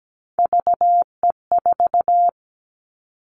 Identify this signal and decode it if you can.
Morse code, heard as a keyed tone, about 17 wpm, 700 Hz. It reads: VE4
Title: Callsign Prefixes for Top 100 DXCC - 17wpm